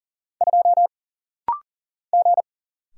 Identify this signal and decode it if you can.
Morse code, heard as a keyed tone, about 40 wpm, 700 Hz. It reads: 2 G
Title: Single Letter-Number Alt - Rapid-Fire 40wpm